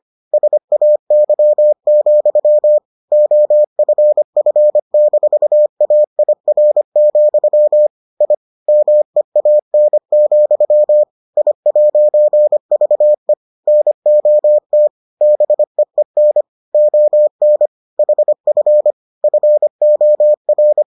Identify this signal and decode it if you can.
Morse code, heard as a keyed tone, about 25 wpm, 600 Hz. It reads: SAY, OFF-AIR, I MEAN, I'VE NOT BEEN ON HF FOR